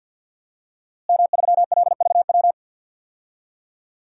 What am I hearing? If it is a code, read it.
Morse code, heard as a keyed tone, about 52 wpm, 700 Hz. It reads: M3LVW
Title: DXCC-100 Call Signs - 5 Characters - Repeated 3x - 50wpm